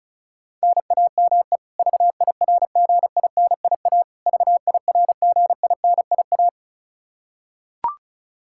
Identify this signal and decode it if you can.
Morse code, heard as a keyed tone, about 34 wpm, 700 Hz. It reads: NAME VIRGINIA VIRGINIA
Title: Practice QSO - Exchange 1 - Beginning - 35wpm